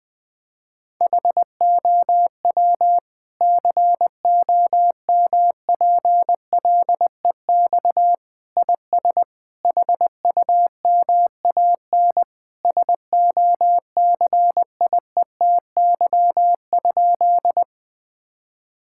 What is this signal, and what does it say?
Morse code, heard as a keyed tone, about 20 wpm, 700 Hz. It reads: HOW COMPLEX IS HUMAN SOCIETY?
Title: Sentences from Top 1000 Words - Review Part 6 - 20wpm